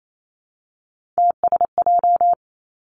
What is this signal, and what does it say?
Morse code, heard as a keyed tone, about 28 wpm, 700 Hz. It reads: TSJ
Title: Three Challenging Letter-Number Combinations 28wpm